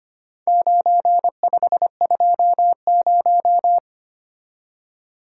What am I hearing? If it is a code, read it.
Morse code, heard as a keyed tone, about 25 wpm, 700 Hz. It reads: 9520
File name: practice / 25wpm / Four Letter-Number Combinations - 25wpm